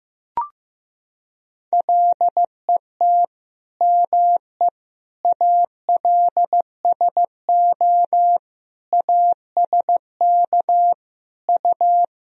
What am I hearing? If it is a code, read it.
Morse code, heard as a keyed tone, about 16 wpm, 700 Hz. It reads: LET ME ALSO ASK U
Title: Sentences from Top 400 Words and Abbreviations 15wpm